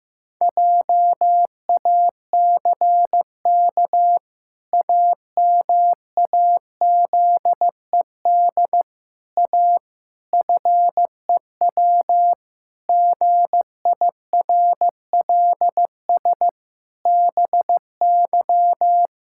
Morse code, heard as a keyed tone, about 15 wpm, 700 Hz. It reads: JACK AMAZED A FEW GIRLS BY
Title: Pangram Challenge v2 - 15wpm